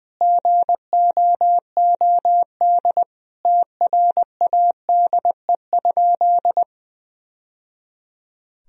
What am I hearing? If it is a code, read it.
Morse code, heard as a keyed tone, about 21 wpm, 700 Hz. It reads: GOOD TRADE?
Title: Sentences from Top 1000 Words - Review Part 4 - 20wpm